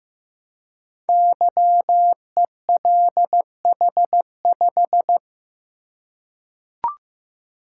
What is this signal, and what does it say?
Morse code, heard as a keyed tone, about 15 wpm, 700 Hz. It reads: YELH5
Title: Code Groups - 5 Letter:Numbers - 15wpm